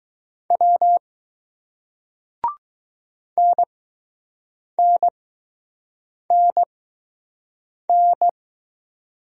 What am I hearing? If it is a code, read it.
Morse code, heard as a keyed tone, about 20 wpm, spaced as for 6 wpm, 700 Hz. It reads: W NNNN